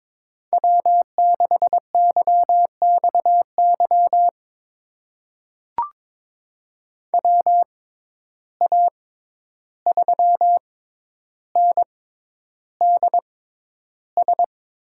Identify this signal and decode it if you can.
Morse code, heard as a keyed tone, about 22 wpm, spaced as for 11 wpm, 700 Hz. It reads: W6YXY WA3NDS
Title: USA Call Signs - 6x Character Spacing - 22wpm